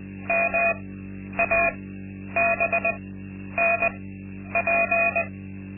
Morse code, heard as a keyed tone, about 20 wpm, spaced as for 10 wpm, 680 Hz. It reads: MABNP
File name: Linksmakalnis-2